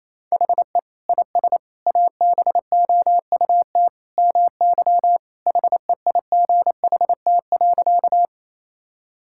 Morse code, heard as a keyed tone, about 28 wpm, 700 Hz. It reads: HE IS ABOUT MY HEIGHT.
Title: Sentences from Top 1600 Words 28wpm